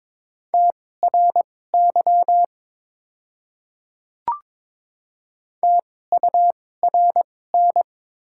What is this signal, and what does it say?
Morse code, heard as a keyed tone, about 22 wpm, spaced as for 16 wpm, 700 Hz. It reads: TRY TURN
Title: 2x Character Spacing - Top 200 Words - 22wpm